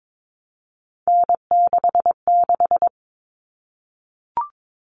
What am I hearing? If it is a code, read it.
Morse code, heard as a keyed tone, about 22 wpm, 700 Hz. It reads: N66
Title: All Cut Numbers - 3-Digit 22wpm